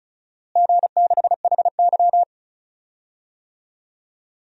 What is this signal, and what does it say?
Morse code, heard as a keyed tone, about 34 wpm, 700 Hz. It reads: G6HY
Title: DXCC-100 Call Signs - 4 Characters - Repeated 2x - 35wpm